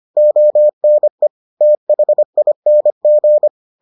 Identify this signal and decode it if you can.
Morse code, heard as a keyed tone, about 25 wpm, 600 Hz. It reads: ONE THING